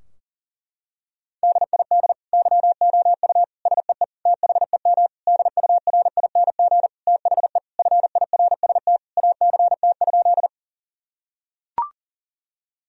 Morse code, heard as a keyed tone, about 40 wpm, 700 Hz. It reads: DID YOU SEE THEM DURING THE FIRST ACT?
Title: Sentences from Top 1000 Words - Review Part 2 - 40wpm